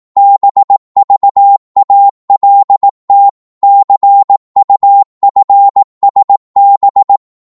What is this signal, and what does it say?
Morse code, heard as a keyed tone, about 18 wpm, 800 Hz. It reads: BVALT CUFSB
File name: Morse / ltr-18wpm